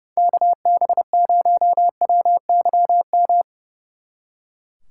Morse code, heard as a keyed tone, about 37 wpm, 700 Hz. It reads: KB0WYM
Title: Speed Racing - USA 2x3 Call Signs 30wpm